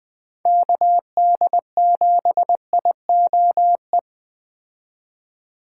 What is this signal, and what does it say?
Morse code, heard as a keyed tone, about 20 wpm, 700 Hz. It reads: KD7IOE
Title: Speed Racing - USA 2x3 Call Signs 17wpm